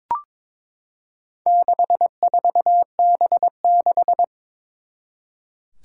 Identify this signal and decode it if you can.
Morse code, heard as a keyed tone, about 22 wpm, 700 Hz. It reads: 64B6
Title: Four Challenging Letter-Number Combinations 22wpm